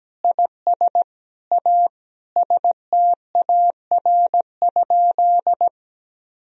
Morse code, heard as a keyed tone, about 17 wpm, 700 Hz. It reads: IS A STAR?
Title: Sentences from Top 1000 Words - Review Part 2 - 17wpm